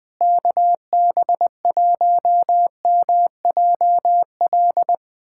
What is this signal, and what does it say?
Morse code, heard as a keyed tone, about 20 wpm, 700 Hz. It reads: KB1MJL
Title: USA Callsign DE Callsign - 6 Characters - 20wpm